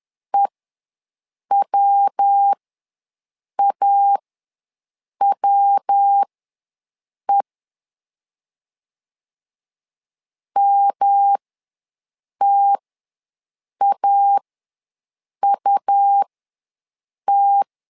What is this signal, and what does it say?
Morse code, heard as a keyed tone, about 10 wpm, spaced as for 6 wpm, 800 Hz. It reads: EWAWE MTAUT
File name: LEZIONI / 3 MWRO / 3 MW ws 10wpm 2213